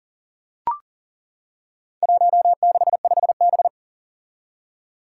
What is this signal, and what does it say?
Morse code, heard as a keyed tone, about 40 wpm, 700 Hz. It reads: 165B
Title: Four Challenging Letter-Number Combinations 40wpm